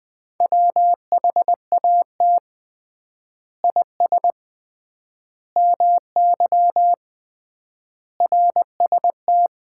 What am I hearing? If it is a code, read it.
Morse code, heard as a keyed tone, about 20 wpm, 700 Hz. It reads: WHAT IS MY RST
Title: Common QSO Phrases - 3x Word-Spacing - 20wpm